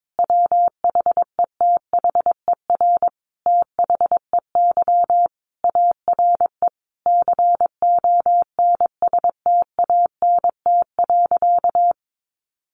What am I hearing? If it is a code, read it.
Morse code, heard as a keyed tone, about 22 wpm, 700 Hz. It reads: WHETHER THEY ARE CONSTANT.
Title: Sentences from Top 1500 Words 22wpm